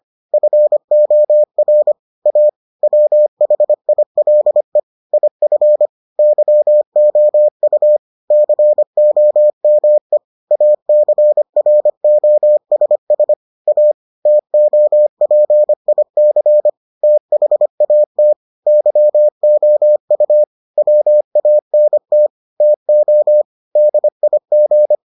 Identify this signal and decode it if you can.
Morse code, heard as a keyed tone, about 25 wpm, 600 Hz. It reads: FOR A WHILE IF YOU COME ACROSS A TOPIC THAT YOU WANT TO DIG